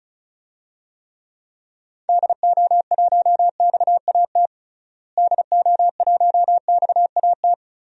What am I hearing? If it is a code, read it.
Morse code, heard as a keyed tone, about 36 wpm, 700 Hz. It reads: DO1XAT DO1XAT
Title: DXCC-100 Call Signs - 6 Characters - Repeated 2x - 35wpm